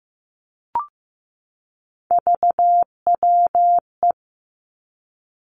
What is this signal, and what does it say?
Morse code, heard as a keyed tone, about 15 wpm, 700 Hz. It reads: VWE
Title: Three Challenging Letter-Number Combinations 15wpm